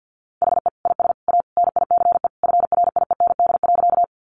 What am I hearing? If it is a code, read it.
Morse code, heard as a keyed tone, about 50 wpm, 700 Hz. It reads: HE IS A NICE FRIEND.
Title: Sentences from Top 1100 Words 50wpm